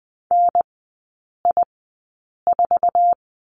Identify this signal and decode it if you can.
Morse code, heard as a keyed tone, about 20 wpm, spaced as for 8 wpm, 700 Hz. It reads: NI4
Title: 047 Lesson 9e - Number 4 - Three Characters 20wpm